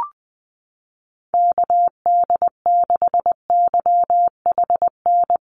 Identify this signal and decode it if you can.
Morse code, heard as a keyed tone, about 20 wpm, 700 Hz. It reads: KD6YHN
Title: USA Call Signs 20wpm